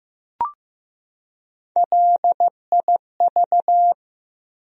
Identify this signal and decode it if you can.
Morse code, heard as a keyed tone, about 15 wpm, 700 Hz. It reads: LIV
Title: USA Common Names - Max 3 Letters - 15wpm